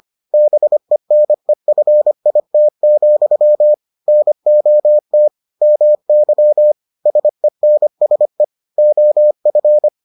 Morse code, heard as a keyed tone, about 25 wpm, 600 Hz. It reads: BENEFIT, NOT MY SENSE OF